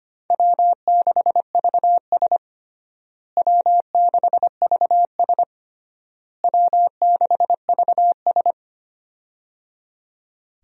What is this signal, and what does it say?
Morse code, heard as a keyed tone, about 25 wpm, 700 Hz. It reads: W6VS W6VS W6VS
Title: USA Call Signs - 4 Characters - Repeated 3x - 25wpm